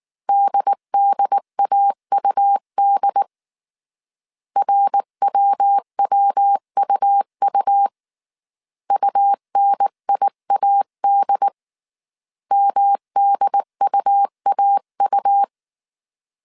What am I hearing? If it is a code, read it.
Morse code, heard as a keyed tone, about 19 wpm, 800 Hz. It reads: DDAUD RWWUU UNIAD MDUAU